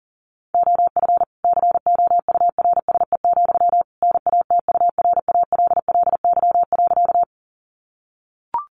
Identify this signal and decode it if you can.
Morse code, heard as a keyed tone, about 40 wpm, 700 Hz. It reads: OF COURSE, NATURALLY.
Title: Sentences from Top 1300 Words 40wpm